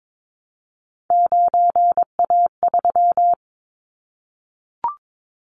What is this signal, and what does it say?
Morse code, heard as a keyed tone, about 22 wpm, 700 Hz. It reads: 9A3
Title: Callsign Prefixes for Top 50 DXCC - 22wpm